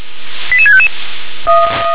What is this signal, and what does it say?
Five-tone selective call (ZVEI), heard as quick runs of five tones, tones of about 70 ms. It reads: 88655